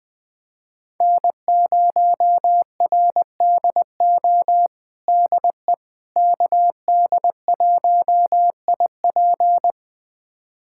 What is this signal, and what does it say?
Morse code, heard as a keyed tone, about 20 wpm, 700 Hz. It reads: N0RDO DE KD1IP
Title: USA Callsign DE Callsign - 5 Characters - 20wpm